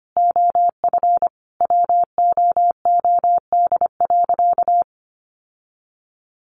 Morse code, heard as a keyed tone, about 25 wpm, 700 Hz. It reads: OF WOOD.
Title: Sentences from Top 1500 Words 25wpm